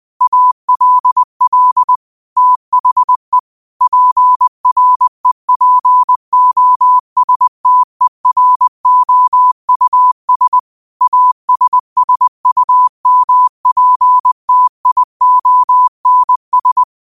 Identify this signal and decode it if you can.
Morse code, heard as a keyed tone, about 20 wpm, 1000 Hz. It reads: ALL THE PREPOSTEROUS ASSUMPTIONS